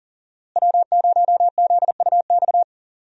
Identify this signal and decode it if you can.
Morse code, heard as a keyed tone, about 40 wpm, 700 Hz. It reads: W0ZUX
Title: USA Call Signs - 5 Characters - Repeated 2x - 40wpm